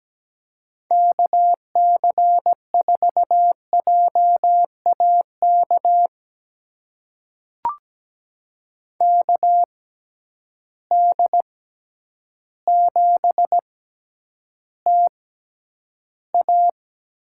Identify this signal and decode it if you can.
Morse code, heard as a keyed tone, about 17 wpm, spaced as for 9 wpm, 700 Hz. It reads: KC4JAK KD7TA
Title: USA Call Signs - 6 Characters - 6x Character Spacing - 17wpm